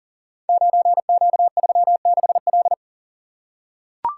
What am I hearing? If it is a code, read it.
Morse code, heard as a keyed tone, about 40 wpm, 690 Hz. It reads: 9Q36L
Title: Code Groups - 5 Letter:Numbers - 40wpm